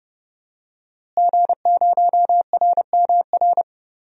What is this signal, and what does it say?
Morse code, heard as a keyed tone, about 29 wpm, 700 Hz. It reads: G0RMR
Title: DXCC-100 Call Signs - 5 Characters - Repeated 3x - 30wpm